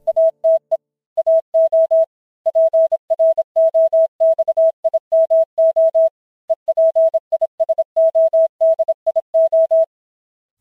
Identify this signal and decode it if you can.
Morse code, heard as a keyed tone, about 26 wpm, 650 Hz. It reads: ATE AO PROXIMO EPISODIO